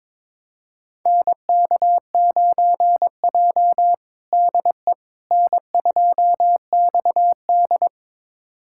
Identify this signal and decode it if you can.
Morse code, heard as a keyed tone, about 22 wpm, 700 Hz. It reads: NK9J DE N2XD
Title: USA Callsign DE Callsign - 4 Characters - 22wpm